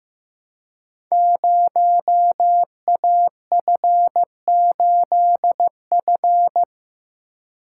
Morse code, heard as a keyed tone, about 15 wpm, 700 Hz. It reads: 0AF8F
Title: Code Groups - 5 Letter:Numbers - 15wpm